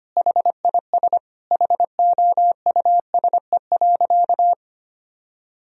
Morse code, heard as a keyed tone, about 25 wpm, 700 Hz. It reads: HIS HOUSE.
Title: Sentences from Top 1000 Words - Review Part 1 - 25wpm